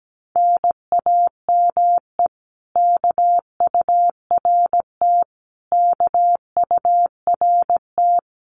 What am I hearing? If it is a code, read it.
Morse code, heard as a keyed tone, about 17 wpm, 700 Hz. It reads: NAME KURT KURT
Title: Common QSO Phrases Encore 17wpm